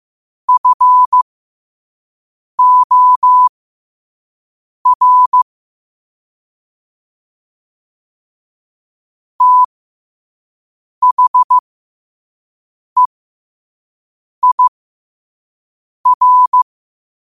Morse code, heard as a keyed tone, about 15 wpm, spaced as for 5 wpm, 1000 Hz. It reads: FOR THEIR
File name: qotdfiles / 5wpm